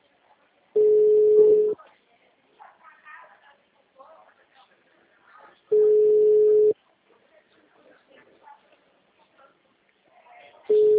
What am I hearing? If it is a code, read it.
Ringback tone (Europe, 425 Hz), with speech.